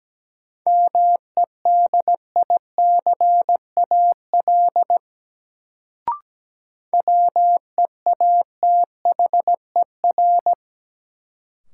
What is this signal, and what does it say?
Morse code, heard as a keyed tone, about 17 wpm, 700 Hz. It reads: MEDICAL WEATHER
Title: Sentences from Top 800 Words - New Word Drill - 17wpm